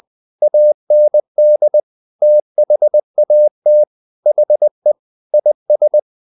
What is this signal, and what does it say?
Morse code, heard as a keyed tone, about 20 wpm, 600 Hz. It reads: AND THAT HE IS
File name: the old man and the sea part 2 20wpm